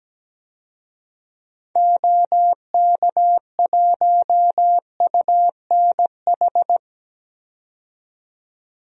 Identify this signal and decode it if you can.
Morse code, heard as a keyed tone, about 17 wpm, 700 Hz. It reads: OK1UNH
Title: DXCC-100 Call Signs - 6 Characters - 17wpm